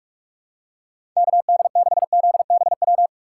Morse code, heard as a keyed tone, about 46 wpm, 700 Hz. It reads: KD6ZBW
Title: USA Call Signs - 2x Character Spacing - 45wpm